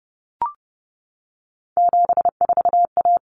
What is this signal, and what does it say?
Morse code, heard as a keyed tone, about 30 wpm, 700 Hz. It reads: 74A
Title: Common Cut Numbers - 3-Digit 30wpm